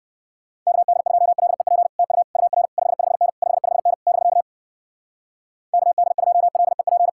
Morse code, heard as a keyed tone, about 66 wpm, 700 Hz. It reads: KD2LEP TU UR 55N 55N <BK> KD2LEP